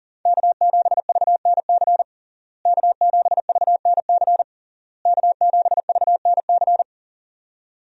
Morse code, heard as a keyed tone, about 37 wpm, 700 Hz. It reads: K7VNC K7VNC K7VNC
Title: USA Call Signs - 5 Characters - Repeated 3x - 40wpm